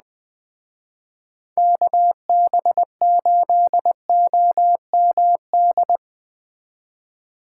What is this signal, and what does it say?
Morse code, heard as a keyed tone, about 19 wpm, 700 Hz. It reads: KB8OMD
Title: Speed Racing - USA 2x3 Call Signs 17wpm